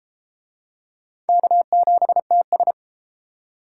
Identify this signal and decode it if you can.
Morse code, heard as a keyed tone, about 33 wpm, 700 Hz. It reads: K7TS